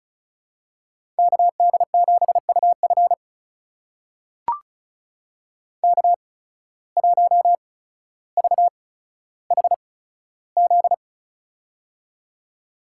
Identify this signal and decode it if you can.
Morse code, heard as a keyed tone, about 34 wpm, spaced as for 15 wpm, 700 Hz. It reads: KD7UF K1VHZ
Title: USA Call Signs - 5 Characters - 8x Character Spacing - 35wpm